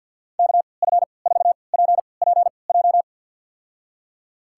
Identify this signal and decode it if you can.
Morse code, heard as a keyed tone, about 46 wpm, spaced as for 27 wpm, 700 Hz. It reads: KF4PPJ
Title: USA Call Signs - 3x Character Spacing - 50wpm